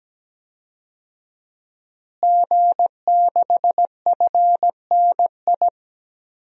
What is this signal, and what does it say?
Morse code, heard as a keyed tone, about 17 wpm, 700 Hz. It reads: G6FNI
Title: DXCC-100 Call Signs - 5 Characters - Repeated 3x - 17wpm